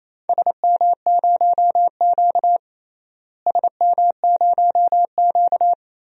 Morse code, heard as a keyed tone, about 28 wpm, 700 Hz. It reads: SM0Q SM0Q
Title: DXCC-100 Call Signs - 4 Characters - Repeated 2x - 28wpm